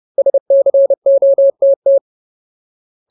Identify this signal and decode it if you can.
Morse code, heard as a keyed tone, about 30 wpm, 550 Hz. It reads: SCOTT
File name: Ultimate Word List / 30WPM Ultimate